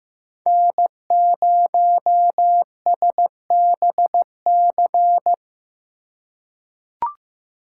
Morse code, heard as a keyed tone, about 16 wpm, 700 Hz. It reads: N0SBC
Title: USA Call Signs - 5 Characters - 4x Character Spacing - 15wpm